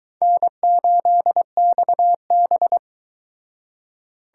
Morse code, heard as a keyed tone, about 23 wpm, 700 Hz. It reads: N8XB